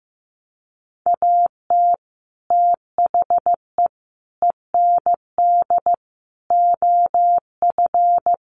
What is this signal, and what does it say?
Morse code, heard as a keyed tone, about 15 wpm, 700 Hz. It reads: AT THE END OF
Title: Sets of 5 Words 15wpm